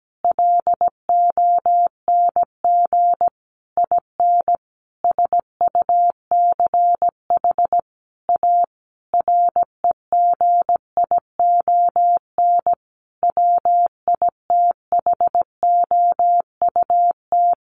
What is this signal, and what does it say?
Morse code, heard as a keyed tone, about 17 wpm, 700 Hz. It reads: LONG IN SUCH A REGION WITHOUT
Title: Sentences from Top 1500 Words 17wpm